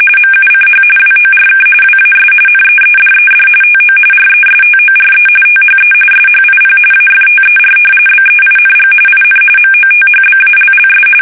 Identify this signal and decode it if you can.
RTTY, about 45.45 baud. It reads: MUCH WORK WAS NEEDED TO RESTORE THE SHIP AFTER BEING IN MOTHBALLS
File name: fsk850 / battleshipnj